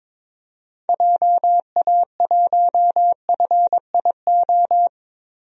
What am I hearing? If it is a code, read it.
Morse code, heard as a keyed tone, about 22 wpm, 700 Hz. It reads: JA1FIO
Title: DXCC-100 Call Signs - 6 Characters - 22wpm